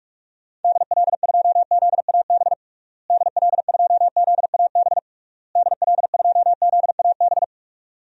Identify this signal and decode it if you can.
Morse code, heard as a keyed tone, about 46 wpm, 700 Hz. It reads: DL2ZAB DL2ZAB DL2ZAB
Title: DXCC-100 Call Signs - 6 Characters - Repeated 3x - 45wpm